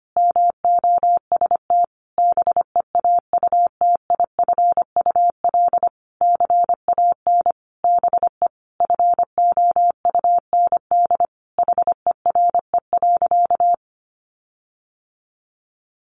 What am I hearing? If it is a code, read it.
Morse code, heard as a keyed tone, about 25 wpm, 700 Hz. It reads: MOST BEAUTIFUL CAN BE FOUND HERE.
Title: Sentences from Top 1500 Words 25wpm